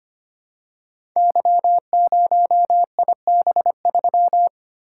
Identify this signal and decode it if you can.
Morse code, heard as a keyed tone, about 25 wpm, 700 Hz. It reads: Y0IB3
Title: Code Groups - 5 Letter:Numbers - 25wpm